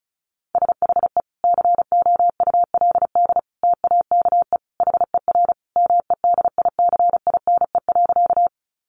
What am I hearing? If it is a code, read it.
Morse code, heard as a keyed tone, about 34 wpm, 700 Hz. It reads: SHE COULD TAKE HER MEDICINE.
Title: Sentences from Top 900 Words 35wpm